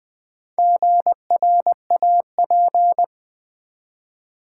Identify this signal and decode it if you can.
Morse code, heard as a keyed tone, about 20 wpm, 700 Hz. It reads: GRAP
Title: Build A Word - 4 Letters At A Time - 8-Letter Words - 20wpm